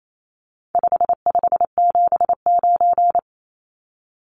Morse code, heard as a keyed tone, about 28 wpm, 700 Hz. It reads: 5579
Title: Numbers - 4-Digit 28wpm